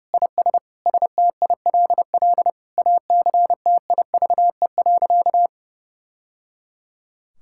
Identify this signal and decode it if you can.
Morse code, heard as a keyed tone, about 30 wpm, 700 Hz. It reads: IS STILL ACTIVE.
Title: Sentences from Top 1000 Words - Review Part 6 - 30wpm